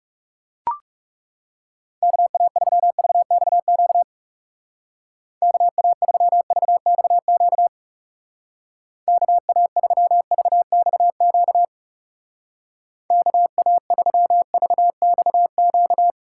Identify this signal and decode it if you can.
Morse code, heard as a keyed tone, about 37 wpm, 700 Hz. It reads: KA3VXQ KA3VXQ KA3VXQ KA3VXQ
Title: Speed Racing - USA Call Signs 30wpm